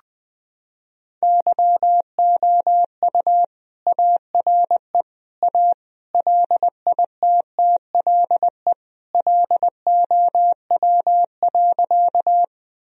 Morse code, heard as a keyed tone, about 20 wpm, 700 Hz. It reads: YOU ARE A LITTLE LOW.
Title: Sentences from Top 1000 Words - Review Part 1 - 20wpm